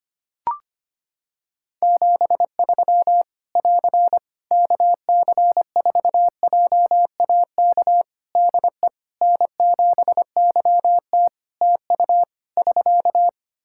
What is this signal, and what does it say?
Morse code, heard as a keyed tone, about 25 wpm, 700 Hz. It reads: 73 <AR> KC4JAK DE N7YT TU <SK>
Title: Practice QSO - Exchange 4 - Ending - 25wpm